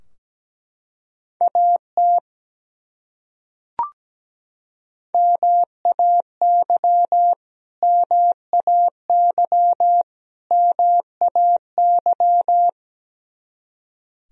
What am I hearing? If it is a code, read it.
Morse code, heard as a keyed tone, about 17 wpm, 700 Hz. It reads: AT MAY MAY MAY
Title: Top 100 Words Repeated 17wpm